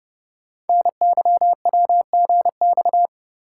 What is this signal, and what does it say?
Morse code, heard as a keyed tone, about 29 wpm, 700 Hz. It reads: NYWGX
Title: Code Groups - 5 Letter:Numbers - 30wpm